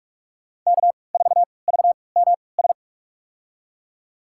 Morse code, heard as a keyed tone, about 42 wpm, spaced as for 25 wpm, 700 Hz. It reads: K4VMS
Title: USA Call Signs - 5 Characters - 3x Character Spacing - 45wpm